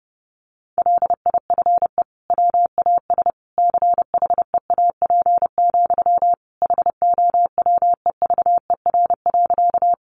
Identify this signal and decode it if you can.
Morse code, heard as a keyed tone, about 30 wpm, 700 Hz. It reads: LIFE WAS CHEAP, HOWEVER.
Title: Sentences from Top 1200 Words 30wpm